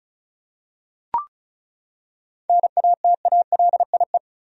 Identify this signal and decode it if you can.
Morse code, heard as a keyed tone, about 36 wpm, 700 Hz. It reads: NATALIE
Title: USA Common Names - 35wpm